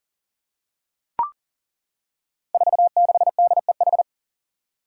Morse code, heard as a keyed tone, about 40 wpm, 700 Hz. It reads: 46DEH
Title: Five Challenging Letter-Number Combinations 40wpm